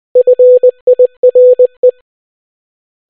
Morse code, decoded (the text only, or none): FIRE